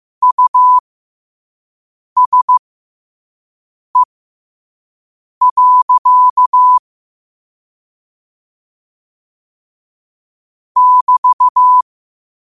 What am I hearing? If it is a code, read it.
Morse code, heard as a keyed tone, about 15 wpm, spaced as for 5 wpm, 1000 Hz. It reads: USE. <BT>
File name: qotdfiles / 5wpm